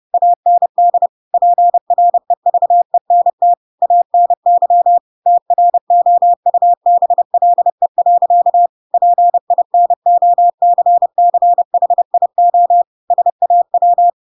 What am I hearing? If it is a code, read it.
Morse code, heard as a keyed tone, about 30 wpm, 700 Hz. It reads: AND PREVENT ANY TROUBLE. PINOCCHIO SAW
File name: Pinocchio / Pinocchio ch3 30 30 ik1hge